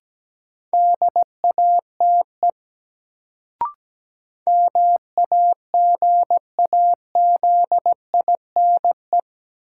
Morse code, heard as a keyed tone, about 17 wpm, 700 Hz. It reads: DATE MAGAZINE
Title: Sentences from Top 1000 Words - Review Part 7 - New Word Drill - 17wpm